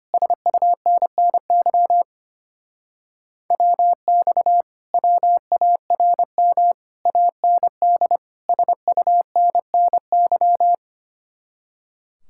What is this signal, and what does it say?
Morse code, heard as a keyed tone, about 29 wpm, 700 Hz. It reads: SUNNY WX WARM AND SUNNY
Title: Speed-Racing Common QSO Phrases - 25wpm